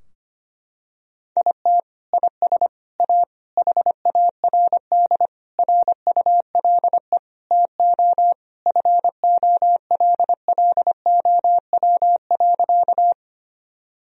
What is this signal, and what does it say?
Morse code, heard as a keyed tone, about 25 wpm, 700 Hz. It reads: IT IS A HARD RULE TO FOLLOW.
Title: Sentences from Top 1000 Words - Review Part 2 - 25wpm